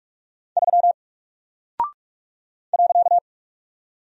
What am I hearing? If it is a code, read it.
Morse code, heard as a keyed tone, about 44 wpm, 700 Hz. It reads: 3 .
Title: Single Character (Letter:Number:Punctuation:Prosign) 45wpm